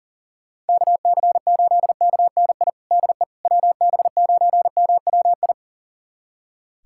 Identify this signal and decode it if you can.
Morse code, heard as a keyed tone, about 40 wpm, 700 Hz. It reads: KC8KNI DE WB9MWI
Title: USA Callsign DE Callsign 40wpm